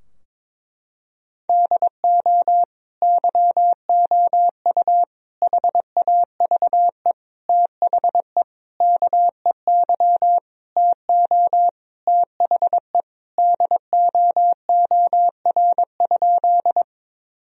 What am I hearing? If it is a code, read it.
Morse code, heard as a keyed tone, about 22 wpm, 700 Hz. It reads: DO YOU HAVE THE KEY TO THE DOOR?
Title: Sentences from Top 1000 Words - Review Part 4 - 22wpm